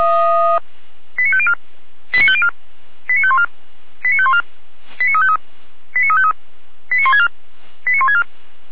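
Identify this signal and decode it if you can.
Five-tone selective call (ZVEI), heard as quick runs of five tones, tones of about 70 ms. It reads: 89504 89504 89514 89514 89353 89353 89164 89164
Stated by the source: ZVEI five-tone selective call received on an SDR.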